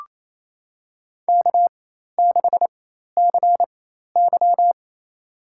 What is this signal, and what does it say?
Morse code, heard as a keyed tone, about 27 wpm, spaced as for 13 wpm, 700 Hz. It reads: K6CY
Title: USA Call Signs - 4 Characters - 4x Character Spacing - 28wpm